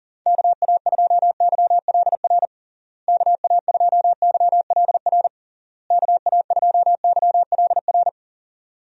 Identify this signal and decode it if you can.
Morse code, heard as a keyed tone, about 40 wpm, 700 Hz. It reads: KA2YLR KA2YLR KA2YLR
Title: USA Call Signs - 6 Characters - Repeated 3x - 40wpm